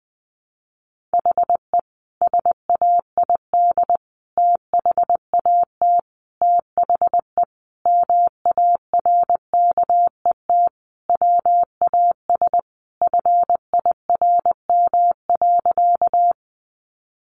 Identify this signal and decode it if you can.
Morse code, heard as a keyed tone, about 20 wpm, 700 Hz. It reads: HE SAID THAT THE MARKET WAS FIRM.
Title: Sentences from Top 1400 Words 20wpm